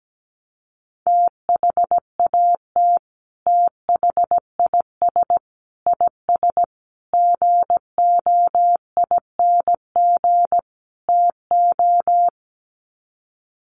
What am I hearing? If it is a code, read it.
Morse code, heard as a keyed tone, about 17 wpm, 700 Hz. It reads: THAT THIS IS GOING TO
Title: Sets of 5 Words Encore2 17wpm